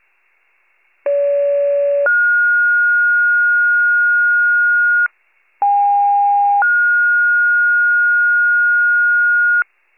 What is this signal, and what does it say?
Two-tone page, heard as a one-second tone, then a three-second tone, about 570 then 1430 Hz, about 810 then 1430 Hz. It is heard from 1.0 to 9.6 s.
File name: wcalert